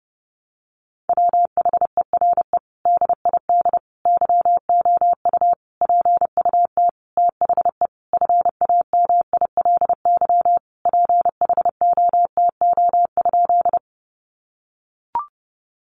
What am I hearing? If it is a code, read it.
Morse code, heard as a keyed tone, about 30 wpm, 700 Hz. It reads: WHERE DID YOU PUT THE FAMILY PHOTO?
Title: Sentences from Top 1200 Words 30wpm